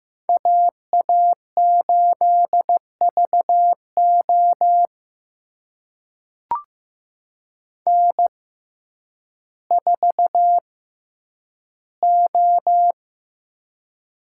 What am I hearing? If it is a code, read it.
Morse code, heard as a keyed tone, about 15 wpm, spaced as for 9 wpm, 700 Hz. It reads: AA8VO N4O
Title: USA Call Signs - 5 Characters - 6x Character Spacing - 15wpm